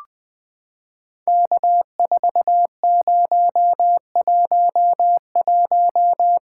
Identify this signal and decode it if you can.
Morse code, heard as a keyed tone, about 20 wpm, 700 Hz. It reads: K4011
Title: POTA QSO Practice - 20wpm